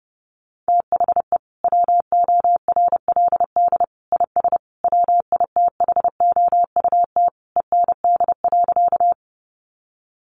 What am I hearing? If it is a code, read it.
Morse code, heard as a keyed tone, about 30 wpm, 700 Hz. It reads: THE WORLD IS WITHOUT END.
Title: Sentences from Top 600 Words 30wpm